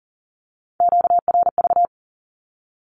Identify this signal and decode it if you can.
Morse code, heard as a keyed tone, about 39 wpm, 700 Hz. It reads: QRV
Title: Q-Codes Repeated 3x 40wpm